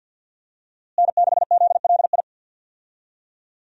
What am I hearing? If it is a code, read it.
Morse code, heard as a keyed tone, about 51 wpm, 700 Hz. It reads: N6ZLI
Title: USA Call Signs - 5 Characters - Repeated 2x - 50wpm